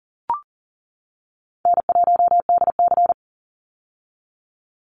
Morse code, heard as a keyed tone, about 41 wpm, 700 Hz. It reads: N1DC
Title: Easy Call Signs - 40wpm